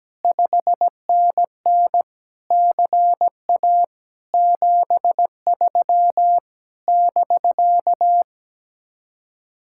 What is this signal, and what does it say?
Morse code, heard as a keyed tone, about 17 wpm, 700 Hz. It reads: 5NN CA 73 <BK>
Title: Speed-Racing Common QSO Phrases - 17wpm